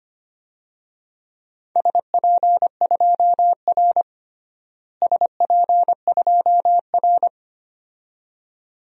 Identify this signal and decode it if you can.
Morse code, heard as a keyed tone, about 25 wpm, 700 Hz. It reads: SP2R SP2R
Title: DXCC-100 Call Signs - 4 Characters - Repeated 2x - 25wpm